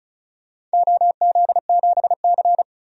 Morse code, heard as a keyed tone, about 34 wpm, 700 Hz. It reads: OZ7C
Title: DXCC-100 Call Signs - 4 Characters - 35wpm